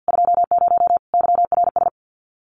Morse code, heard as a keyed tone, about 48 wpm, 700 Hz. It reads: 20 YRS
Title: Common QSO Phrases 50wpm